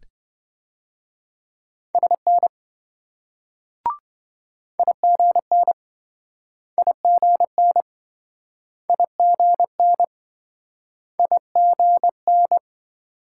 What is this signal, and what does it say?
Morse code, heard as a keyed tone, about 24 wpm, 700 Hz. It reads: SN IGN IGN IGN IGN
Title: Speed-Racing Multi-letter Phonemes - 20wpm